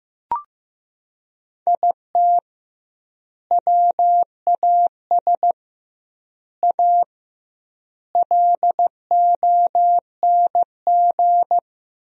Morse code, heard as a keyed tone, about 16 wpm, 700 Hz. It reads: IT WAS A LONG
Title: 2x Word-Spacing - Sentences from Top 500 Words - 15wpm